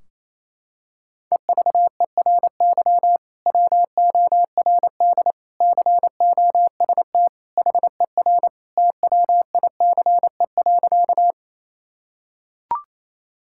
Morse code, heard as a keyed tone, about 28 wpm, 700 Hz. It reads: EVERY WORD COST HER TWICE.
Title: Sentences from Top 1600 Words 28wpm